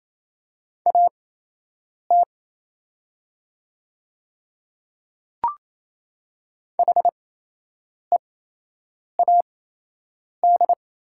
Morse code, heard as a keyed tone, about 27 wpm, spaced as for 8 wpm, 700 Hz. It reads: AT HEAD
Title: 8x Character Spacing - Top 500 Words - 28wpm